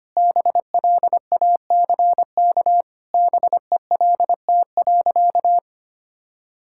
Morse code, heard as a keyed tone, about 25 wpm, 700 Hz. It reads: BLACK BELT.
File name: practice / 25wpm / All-Age Appropriate Jokes - 25wpm